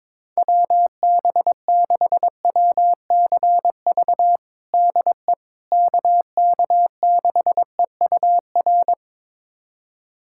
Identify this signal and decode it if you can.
Morse code, heard as a keyed tone, about 22 wpm, 700 Hz. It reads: WB6WCV DE KK6EUR
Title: USA Callsign DE Callsign - 6 Characters - 22wpm